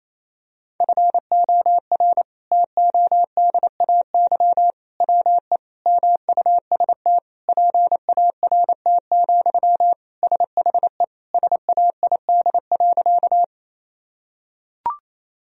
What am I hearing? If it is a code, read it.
Morse code, heard as a keyed tone, about 28 wpm, 700 Hz. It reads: FOR TODAY WE MUST PART, SHE SAID.
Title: Sentences from Top 1000 Words - Review Part 1 - 28wpm